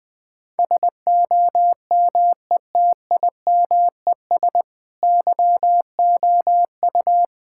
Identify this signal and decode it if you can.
Morse code, heard as a keyed tone, about 20 wpm, 700 Hz. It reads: SOMETIMES YOU
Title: Sentences with 7-Word Phrases - 20wpm